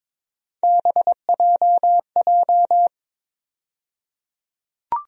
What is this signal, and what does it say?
Morse code, heard as a keyed tone, about 21 wpm, 700 Hz. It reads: BJJ
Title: USA Call Sign Postfixes - 3 Letters 22wpm